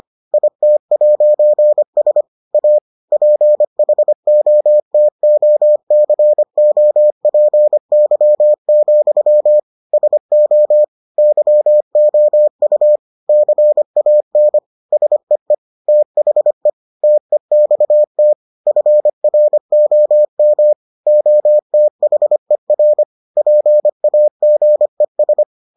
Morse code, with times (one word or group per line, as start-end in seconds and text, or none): 0.3-2.2 IT'S
2.5-2.8 A
3.1-9.6 PHOTOCOPY,
9.9-10.8 SO
11.2-13.0 YOU
13.3-14.6 CAN
14.9-15.5 SEE
15.9-16.7 THE
17.0-18.3 TEXT
18.7-20.7 FROM
21.1-23.0 OTHER
23.4-25.4 PAGES